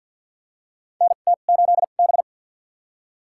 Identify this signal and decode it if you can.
Morse code, heard as a keyed tone, about 50 wpm, spaced as for 36 wpm, 700 Hz. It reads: NT8B
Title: USA Call Signs - 2x Character Spacing - 50wpm